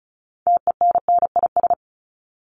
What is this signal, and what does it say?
Morse code, heard as a keyed tone, about 34 wpm, 700 Hz. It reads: TENNIS
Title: Top 1501-1750 Words 35wpm